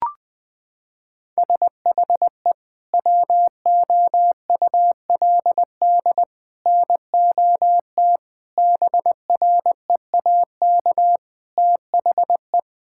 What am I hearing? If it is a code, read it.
Morse code, heard as a keyed tone, about 21 wpm, 700 Hz. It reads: SHE WOULD NOT BREAK THE
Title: Sentences from Top 1000 Words - Review Part 6 - 20wpm